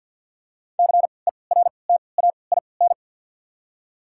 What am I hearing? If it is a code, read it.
Morse code, heard as a keyed tone, about 53 wpm, spaced as for 29 wpm, 690 Hz. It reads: CERTAIN
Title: 3x Character Spacing - Top 500 Words - 50wpm